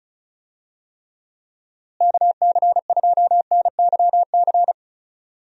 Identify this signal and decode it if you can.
Morse code, heard as a keyed tone, about 34 wpm, 700 Hz. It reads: KC2NYC
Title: USA Call Signs - 6 Characters - Repeated 3x - 35wpm